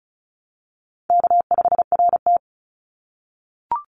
Morse code, heard as a keyed tone, about 36 wpm, 700 Hz. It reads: K5RT
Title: Call Signs - 35wpm